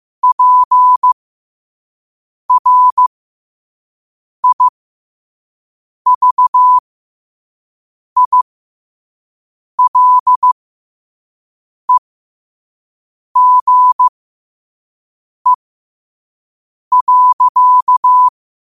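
Morse code, heard as a keyed tone, about 15 wpm, spaced as for 5 wpm, 1000 Hz. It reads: PRIVILEGE.
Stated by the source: Morse code at 5 wpm.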